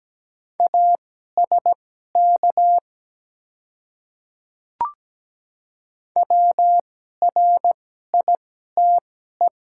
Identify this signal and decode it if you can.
Morse code, heard as a keyed tone, about 17 wpm, spaced as for 12 wpm, 700 Hz. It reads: ASK WRITE
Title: 2x Character Spacing - Top 300 Words - 17wpm